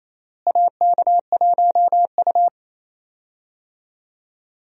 Morse code, heard as a keyed tone, about 27 wpm, 700 Hz. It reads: AK1U
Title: USA Call Signs - 4 Characters - Repeated 2x - 28wpm